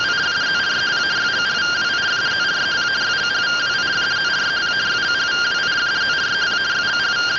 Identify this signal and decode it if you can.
Two-tone teleprinter signal, about 170 Hz shift.
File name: findings / CT 1 Scout 0001.gen-08 rtty rx